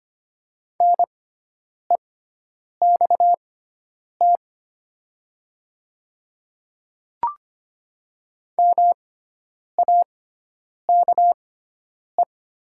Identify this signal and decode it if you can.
Morse code, heard as a keyed tone, about 25 wpm, spaced as for 9 wpm, 700 Hz. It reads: NEXT MAKE
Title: 6x Character Spacing - Top 300 Words - 25wpm